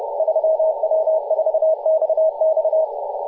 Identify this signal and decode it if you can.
Morse code, heard as a keyed tone, about 30 wpm, 630 Hz, with radio noise. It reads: 3W4XX
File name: posts / 2016-aug-17-0039z-3w4xx-14-022mhz-cw